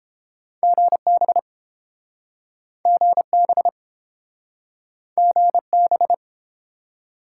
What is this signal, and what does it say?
Morse code, heard as a keyed tone, about 29 wpm, 700 Hz. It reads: GB GB GB